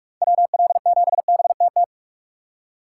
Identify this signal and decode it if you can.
Morse code, heard as a keyed tone, about 44 wpm, 700 Hz. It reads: WL7BTT
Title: USA Call Signs - 6x Character Spacing - 45wpm